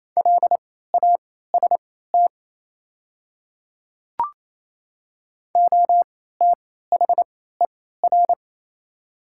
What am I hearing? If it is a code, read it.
Morse code, heard as a keyed tone, about 28 wpm, spaced as for 16 wpm, 700 Hz. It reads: LAST OTHER
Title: 3x Character Spacing - Top 300 Words - 28wpm